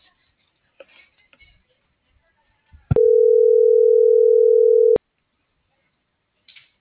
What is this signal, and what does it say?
Ringback tone (North America), 440 + 480 Hz.